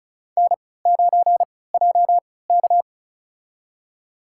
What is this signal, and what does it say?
Morse code, heard as a keyed tone, about 36 wpm, spaced as for 20 wpm, 700 Hz. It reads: N9JK
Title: USA Call Signs - 4 Characters - 3x Character Spacing - 35wpm